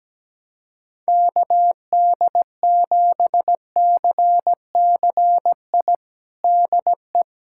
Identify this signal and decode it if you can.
Morse code, heard as a keyed tone, about 17 wpm, 700 Hz. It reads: KD7CCI DE
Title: USA Callsign DE Callsign - 6 Characters - 17wpm